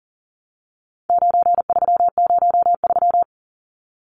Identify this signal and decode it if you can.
Morse code, heard as a keyed tone, about 40 wpm, 700 Hz. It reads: 9303
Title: Numbers - 4-Digit 40wpm